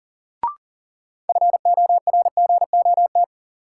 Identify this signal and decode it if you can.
Morse code, heard as a keyed tone, about 40 wpm, 700 Hz. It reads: FORGOT
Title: Sentences from Top 1400 Words - New Word Drill - 40wpm